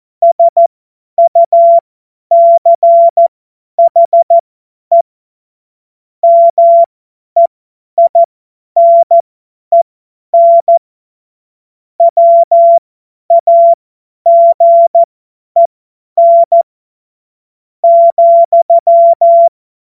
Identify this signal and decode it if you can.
Morse code, heard as a keyed tone, about 14 wpm, spaced as for 10 wpm, 680 Hz. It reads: SUCHE MEINEN WAGEN,
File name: Morsekurs / Witze / Witz 01. Juni 2021, 50 F70